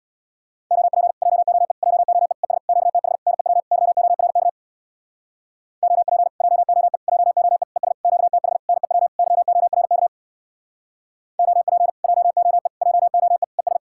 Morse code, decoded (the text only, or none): OP JOE JOE ES QTH NEW YORK OP JOE JOE ES QTH NEW YORK OP JOE JOE ES